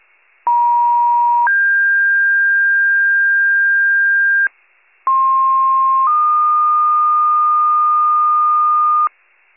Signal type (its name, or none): two-tone page